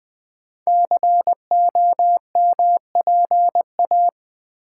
Morse code, heard as a keyed tone, about 20 wpm, 700 Hz. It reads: COMPA
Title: Build A Word - 5 Letters At A Time - 10-Letter Words - 20wpm